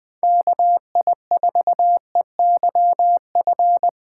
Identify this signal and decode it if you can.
Morse code, heard as a keyed tone, about 20 wpm, 700 Hz. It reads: KI4EYF
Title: Speed Racing - USA 2x3 Call Signs 17wpm